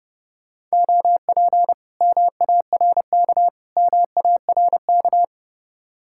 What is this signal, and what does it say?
Morse code, heard as a keyed tone, about 36 wpm, 700 Hz. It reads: OP MARK MARK
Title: Speed-Racing Common QSO Phrases Encore - 30wpm